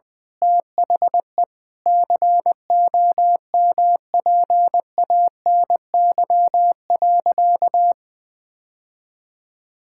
Morse code, heard as a keyed tone, about 20 wpm, 700 Hz. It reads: THE COMPANY.
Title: Sentences with 8-Word Phrases - 20wpm